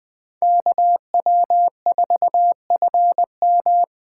Morse code, heard as a keyed tone, about 20 wpm, 700 Hz. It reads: KW4FM
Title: 2023 POTA Hunter Call Signs - 20wpm